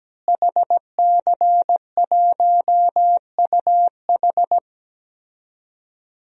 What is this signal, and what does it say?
Morse code, heard as a keyed tone, about 17 wpm, 700 Hz. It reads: HC1UH
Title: DXCC-100 Call Signs - 5 Characters - Repeated 3x - 17wpm